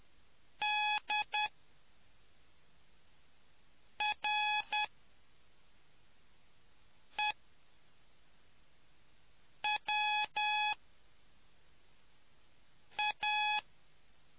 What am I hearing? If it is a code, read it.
Morse code, heard as a keyed tone, about 10 wpm, spaced as for 3 wpm, 840 Hz. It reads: DREWA